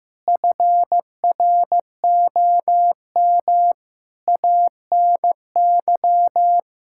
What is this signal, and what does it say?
Morse code, heard as a keyed tone, about 15 wpm, 700 Hz. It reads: FROM ANY
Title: Sentences from Top 1000 Words - Review Part 5 - 15wpm